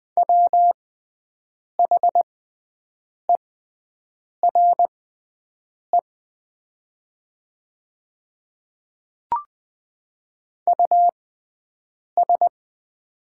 Morse code, heard as a keyed tone, about 20 wpm, spaced as for 7 wpm, 700 Hz. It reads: WHERE US
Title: 6x Character Spacing - Top 200 Words - 20wpm